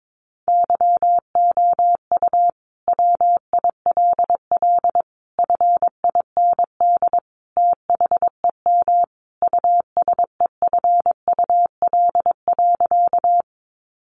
Morse code, heard as a keyed tone, about 22 wpm, 700 Hz. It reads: YOU WILL FIND THEM USEFUL.
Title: Sentences from Top 700 Words 22wpm